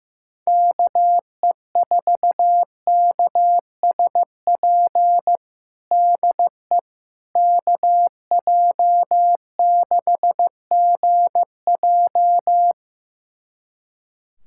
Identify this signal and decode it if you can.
Morse code, heard as a keyed tone, about 15 wpm, 700 Hz. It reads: KE4KSP DE KJ6GJ
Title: USA Callsign DE Callsign 15wpm